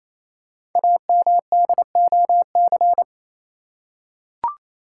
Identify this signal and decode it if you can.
Morse code, heard as a keyed tone, about 27 wpm, 700 Hz. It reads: AMDOC
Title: Code Groups - 5 Letters - 28wpm